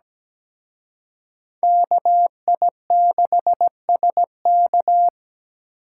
Morse code, heard as a keyed tone, about 20 wpm, 700 Hz. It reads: KI6SK
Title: Speed Racing - USA 2x2 Call Signs 17wpm